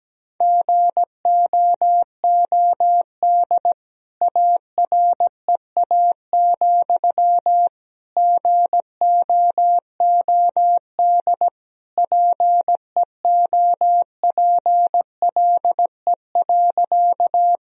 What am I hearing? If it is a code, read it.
Morse code, heard as a keyed tone, about 17 wpm, 700 Hz. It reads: GOOD AREA, GOOD PEOPLE.
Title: Sentences from Top 1000 Words - Review Part 2 - 17wpm